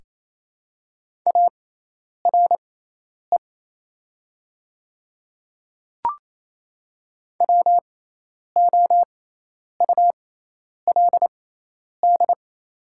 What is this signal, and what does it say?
Morse code, heard as a keyed tone, about 28 wpm, spaced as for 10 wpm, 700 Hz. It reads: ARE WOULD
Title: 6x Character Spacing - Top 100 Words - 28wpm